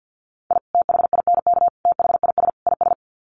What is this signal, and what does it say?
Morse code, heard as a keyed tone, about 50 wpm, 700 Hz. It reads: I THINK THIS IS
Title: Sets of 4 Words 50wpm